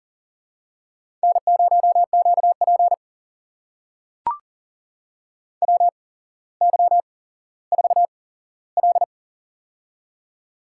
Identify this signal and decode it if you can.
Morse code, heard as a keyed tone, about 38 wpm, spaced as for 17 wpm, 700 Hz. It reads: N0QP WY4L
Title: USA Call Signs - 4 Characters - 8x Character Spacing - 40wpm